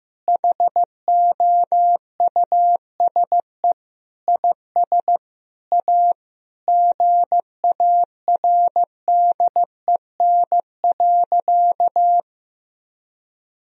Morse code, heard as a keyed tone, about 15 wpm, 700 Hz. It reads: HOUSE IS A GARDEN.
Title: Sentences from Top 1000 Words - Review Part 4 - 15wpm